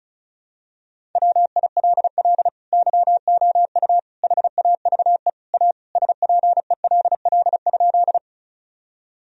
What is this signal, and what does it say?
Morse code, heard as a keyed tone, about 36 wpm, 700 Hz. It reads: WILL YOU HAVE A SPELL?
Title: Sentences from Top 1000 Words - Review Part 1 - 35wpm